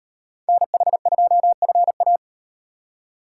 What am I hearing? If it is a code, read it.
Morse code, heard as a keyed tone, about 38 wpm, 700 Hz. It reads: NH2FA